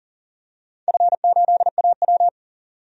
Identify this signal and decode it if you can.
Morse code, heard as a keyed tone, about 39 wpm, 700 Hz. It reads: F8AW
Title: DXCC-100 Call Signs - 4 Characters - 40wpm